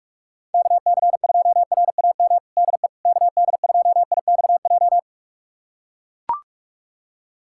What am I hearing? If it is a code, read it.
Morse code, heard as a keyed tone, about 43 wpm, 700 Hz. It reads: KC2RAM DE KD2IXJ
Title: USA Callsign DE Callsign - 6 Characters - 45wpm